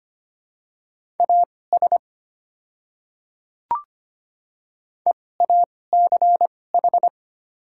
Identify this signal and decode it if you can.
Morse code, heard as a keyed tone, about 25 wpm, spaced as for 18 wpm, 700 Hz. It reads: AS EACH
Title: 2x Character Spacing - Top 200 Words - 25wpm